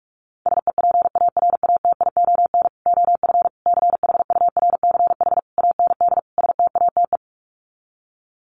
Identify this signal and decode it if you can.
Morse code, heard as a keyed tone, about 44 wpm, 700 Hz. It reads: SEPARATION OF CHURCH AND STATE
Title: Sets of 5 Words 45wpm